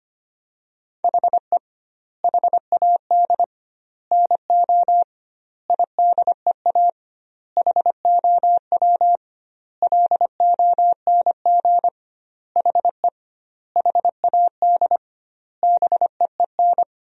Morse code, heard as a keyed tone, about 25 wpm, 700 Hz. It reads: HE HAD NO IDEA HOW LONG HE HAD BEEN
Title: 2x Word-Spacing - Sentences from Top 300 Words - 25wpm